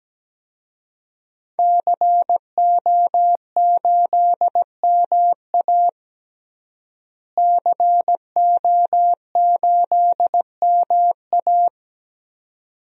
Morse code, heard as a keyed tone, about 17 wpm, 700 Hz. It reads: CO8MA CO8MA
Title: DXCC-100 Call Signs - 5 Characters - Repeated 2x - 17wpm